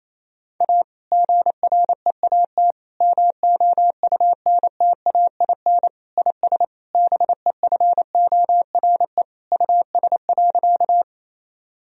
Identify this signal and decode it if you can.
Morse code, heard as a keyed tone, about 28 wpm, 700 Hz. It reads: A GREAT MOUNTAIN IS BEFORE US.
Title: Sentences from Top 1000 Words - Review Part 1 - 28wpm